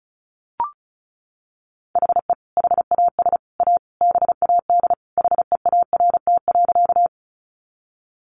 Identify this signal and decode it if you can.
Morse code, heard as a keyed tone, about 34 wpm, 700 Hz. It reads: HE HAS A BAD HEART.
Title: Sentences from Top 1200 Words 35wpm